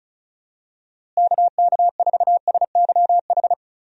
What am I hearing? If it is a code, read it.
Morse code, heard as a keyed tone, about 36 wpm, 700 Hz. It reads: KK4SYH
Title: USA Call Signs - 6 Characters - Repeated 3x - 35wpm